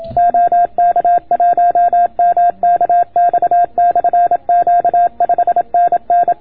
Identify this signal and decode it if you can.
Morse code, heard as a keyed tone, about 27 wpm, 690 Hz. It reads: OK1MKX/Q 5NN